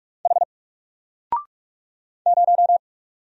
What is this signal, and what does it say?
Morse code, heard as a keyed tone, about 46 wpm, 700 Hz. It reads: H 0
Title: Single Character (Letter:Number:Punctuation:Prosign) 45wpm